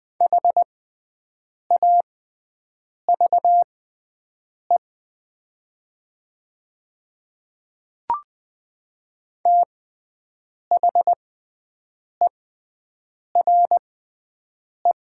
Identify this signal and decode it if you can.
Morse code, heard as a keyed tone, about 20 wpm, spaced as for 7 wpm, 700 Hz. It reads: HAVE THERE
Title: 6x Character Spacing - Top 100 Words - 20wpm